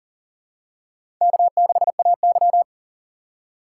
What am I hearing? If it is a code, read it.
Morse code, heard as a keyed tone, about 44 wpm, 700 Hz. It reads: K6AY
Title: USA Call Signs - 4 Characters - 6x Character Spacing - 40wpm